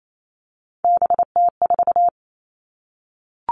Morse code, heard as a keyed tone, about 28 wpm, 700 Hz. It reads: BT4
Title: All Cut Numbers - 3-Digit 28wpm